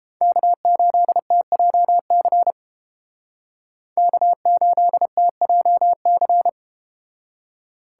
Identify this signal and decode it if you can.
Morse code, heard as a keyed tone, about 32 wpm, 700 Hz. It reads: K8TJC K8TJC